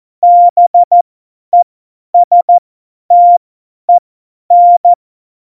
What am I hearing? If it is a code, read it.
Morse code, heard as a keyed tone, about 14 wpm, spaced as for 10 wpm, 700 Hz. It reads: BESTEN
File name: Morsekurs / Witze / Witz 08. Dezember 2020, 50 F70